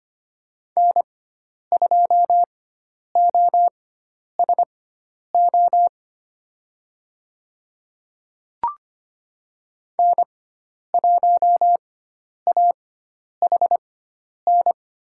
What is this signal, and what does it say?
Morse code, heard as a keyed tone, about 25 wpm, spaced as for 10 wpm, 700 Hz. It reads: N2OSO N1AHN
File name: Lesson 16g - Letter c - Call Signs 25wpm